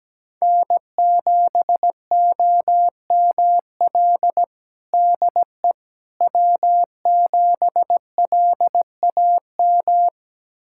Morse code, heard as a keyed tone, about 17 wpm, 700 Hz. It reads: N7OML DE W7LAM
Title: USA Callsign DE Callsign - 5 Characters - 17wpm